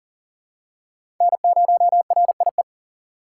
Morse code, heard as a keyed tone, about 40 wpm, 700 Hz. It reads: N0RIE
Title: USA Call Signs - 5 Characters - 4x Character Spacing - 40wpm